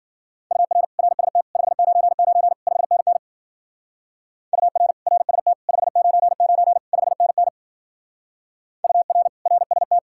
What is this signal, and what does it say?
Morse code, heard as a keyed tone, about 50 wpm, 700 Hz. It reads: UR RST 599 5NN UR RST 599 5NN UR RST